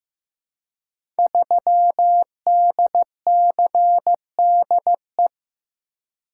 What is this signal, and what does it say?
Morse code, heard as a keyed tone, about 15 wpm, 700 Hz. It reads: 3DCDE
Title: Code Groups - 5 Letter:Numbers - 15wpm